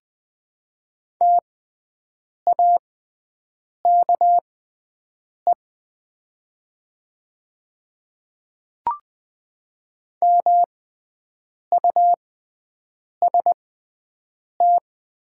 Morse code, heard as a keyed tone, about 20 wpm, spaced as for 7 wpm, 700 Hz. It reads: TAKE MUST
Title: 6x Character Spacing - Top 200 Words - 20wpm